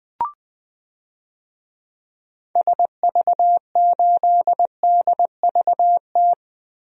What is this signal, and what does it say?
Morse code, heard as a keyed tone, about 21 wpm, 700 Hz. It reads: SV8DVT
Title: DXCC-100 Call Signs - 6 Characters - Repeated 3x - 20wpm